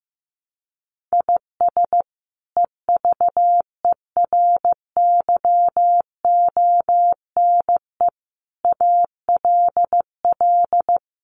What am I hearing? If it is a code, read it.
Morse code, heard as a keyed tone, about 15 wpm, 700 Hz. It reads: IS EVERYONE ALL
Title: Sentences from Top 800 Words 15wpm